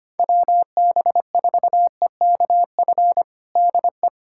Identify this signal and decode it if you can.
Morse code, heard as a keyed tone, about 25 wpm, 700 Hz. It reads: WB4EKF DE
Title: USA Callsign DE Callsign - 6 Characters - 25wpm